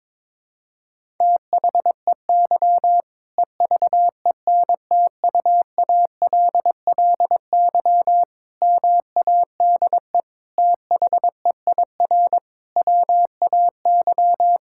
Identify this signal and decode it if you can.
Morse code, heard as a keyed tone, about 22 wpm, 700 Hz. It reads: THEY EVENTUALLY MADE THEIR WAY
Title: Sentences from Top 1000 Words - Review Part 6 - 22wpm